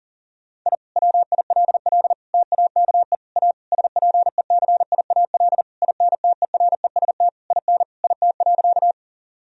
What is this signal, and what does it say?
Morse code, heard as a keyed tone, about 40 wpm, 700 Hz. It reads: I WILL TAKE A SPECIAL INTEREST IN IT.
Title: Sentences from Top 1000 Words - Review Part 3 - 40wpm